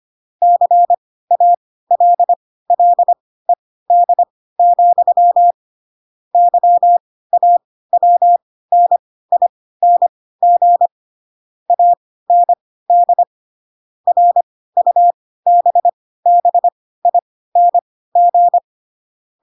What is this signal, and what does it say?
Morse code, heard as a keyed tone, about 25 wpm, spaced as for 16 wpm, 700 Hz. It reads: CALLED, YAWNING AND RUBBING
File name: Pinocchio / Pinocchio ch6 10 25 ik1hge